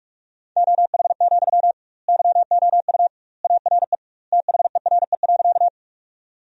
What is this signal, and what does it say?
Morse code, heard as a keyed tone, about 46 wpm, 700 Hz. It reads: OH, YOU ARE THERE.
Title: 2x Word-Spacing - Sentences from Top 500 Words - 45wpm